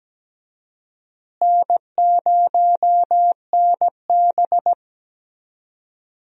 Morse code, heard as a keyed tone, about 17 wpm, 700 Hz. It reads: N0NB
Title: USA Call Signs - 4 Characters - Repeated 3x - 17wpm